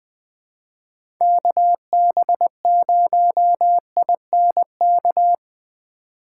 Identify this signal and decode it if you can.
Morse code, heard as a keyed tone, about 20 wpm, 700 Hz. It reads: KB0INK
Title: USA Call Signs - 6 Characters - 4x Character Spacing - 20wpm